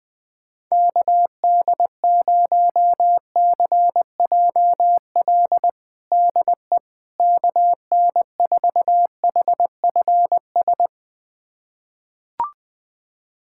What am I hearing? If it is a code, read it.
Morse code, heard as a keyed tone, about 20 wpm, 700 Hz. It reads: KD0CJL DE KN4HFS
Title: USA Callsign DE Callsign 20wpm